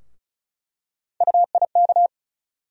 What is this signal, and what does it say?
Morse code, heard as a keyed tone, about 36 wpm, 700 Hz. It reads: UIK
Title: USA Call Sign Postfixes - 3 Letters 35wpm